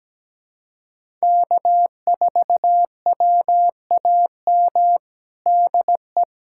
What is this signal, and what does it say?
Morse code, heard as a keyed tone, about 17 wpm, 700 Hz. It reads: K4WAM DE
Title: USA Callsign DE Callsign - 5 Characters - 17wpm